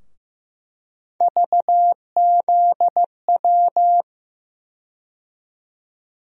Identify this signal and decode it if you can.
Morse code, heard as a keyed tone, about 15 wpm, 700 Hz. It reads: VZW
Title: USA Call Sign Postfixes - 3 Letters 15wpm